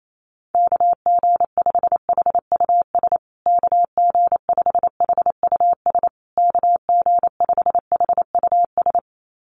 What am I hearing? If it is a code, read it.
Morse code, heard as a keyed tone, about 28 wpm, 700 Hz. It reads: KG5HUS KG5HUS KG5HUS
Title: USA Call Signs Repeated 3x 28wpm